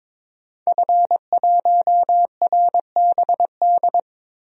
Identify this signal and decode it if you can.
Morse code, heard as a keyed tone, about 22 wpm, 700 Hz. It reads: F1RBD
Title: DXCC-100 Call Signs - 5 Characters - Repeated 2x - 22wpm